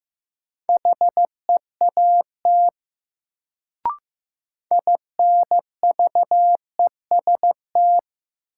Morse code, heard as a keyed tone, about 15 wpm, 700 Hz. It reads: HEAT INVEST
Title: Sentences from Top 1100 Words - New Word Drill Alt - 15wpm